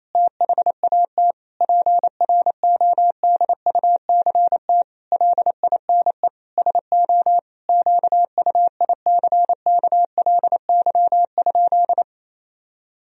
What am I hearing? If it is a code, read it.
Morse code, heard as a keyed tone, about 28 wpm, 700 Hz. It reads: THAT PRODUCT LINE SO QUICKLY?
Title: Sentences from Top 1000 Words - Review Part 3 - 28wpm